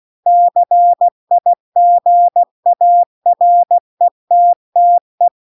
Morse code, heard as a keyed tone, about 16 wpm, 700 Hz. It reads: CIGARETTE
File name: CW-The-Big-Twoheaded-River-2-16WPM